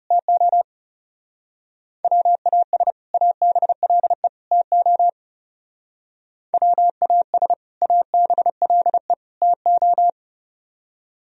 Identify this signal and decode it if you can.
Morse code, heard as a keyed tone, about 31 wpm, 700 Hz. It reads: TO WAS ABLE TO WAS ABLE TO
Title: Speed-Racing - Sets of 3 Words 30wpm